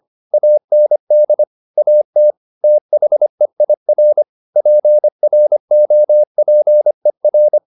Morse code, heard as a keyed tone, about 25 wpm, 600 Hz. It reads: AND AT THEIR PROPER
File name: the old man and the sea part 1 25wpm